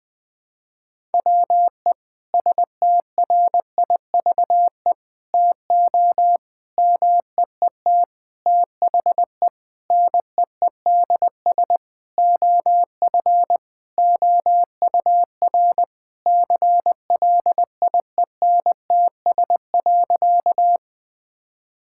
Morse code, heard as a keyed tone, about 21 wpm, 700 Hz. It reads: WE STRIVE TO MEET THE NEEDS OF OUR CLIENTS.
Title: Sentences with 5-Word Phrases - 20wpm